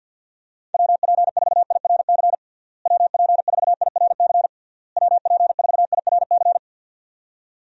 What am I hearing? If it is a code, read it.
Morse code, heard as a keyed tone, about 49 wpm, 700 Hz. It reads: WP4IRC WP4IRC WP4IRC
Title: USA Call Signs - 6 Characters - Repeated 3x - 50wpm